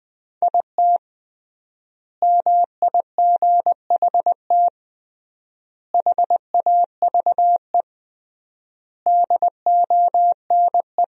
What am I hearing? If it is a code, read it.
Morse code, heard as a keyed tone, about 20 wpm, 700 Hz. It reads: IT MIGHT HAVE DONE
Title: 3x Word-Spacing - Sentences from Top 500 Words - 20wpm